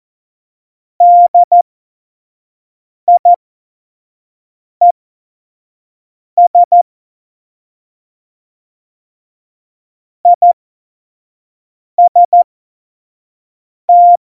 Morse code, heard as a keyed tone, about 14 wpm, spaced as for 5 wpm, 700 Hz. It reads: DIES IST
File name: Morsekurs / Witze / Witz 28 April 2020, 25 F70